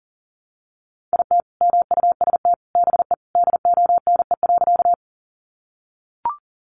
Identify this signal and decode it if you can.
Morse code, heard as a keyed tone, about 40 wpm, 700 Hz. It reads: IT MUST BE DONE.
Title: Sentences from Top 500 Words 40wpm